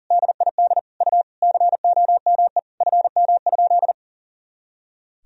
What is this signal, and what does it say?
Morse code, heard as a keyed tone, about 37 wpm, 700 Hz. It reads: DID U COME FM?
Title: Sentences from Top 200 Words and Abbreviations 40wpm